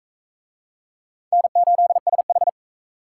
Morse code, heard as a keyed tone, about 41 wpm, 700 Hz. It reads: N8SH